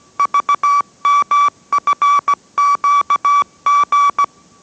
Morse code, heard as a keyed tone, about 17 wpm, 1150 Hz. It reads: VMFQG